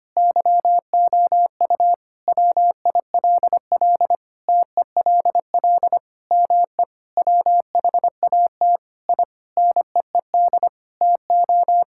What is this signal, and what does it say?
Morse code, heard as a keyed tone, about 25 wpm, 700 Hz. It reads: YOU WILL TELL ME WHAT I NEED TO
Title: Sentences from Top 1000 Words - Review Part 5 - 25wpm